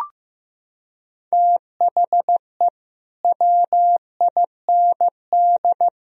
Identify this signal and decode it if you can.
Morse code, heard as a keyed tone, about 15 wpm, 700 Hz. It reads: THE WIND
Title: Sentences from Top 1000 Words - Review Part 2 - 15wpm